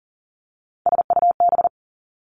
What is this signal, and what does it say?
Morse code, heard as a keyed tone, about 38 wpm, 700 Hz. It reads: SUB
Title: Three Challenging Letter-Number Combinations 40wpm